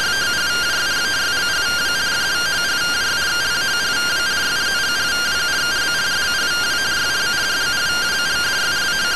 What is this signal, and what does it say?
Two-tone teleprinter signal, about 170 Hz shift.